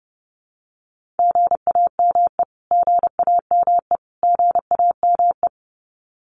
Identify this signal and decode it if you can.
Morse code, heard as a keyed tone, about 30 wpm, 700 Hz. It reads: GAME GAME GAME
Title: Top 500 Words Repeated 30wpm